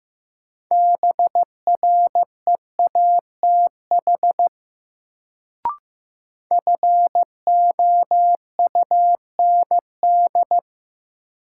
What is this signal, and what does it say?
Morse code, heard as a keyed tone, about 15 wpm, 700 Hz. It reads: BREATH FOUND
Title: Sets of 5 Words Encore - New Word Drill - 15wpm